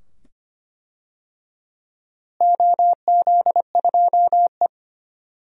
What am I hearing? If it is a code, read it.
Morse code, heard as a keyed tone, about 25 wpm, 700 Hz. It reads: OZ2E
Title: DXCC-100 Call Signs - 4 Characters - Repeated 2x - 25wpm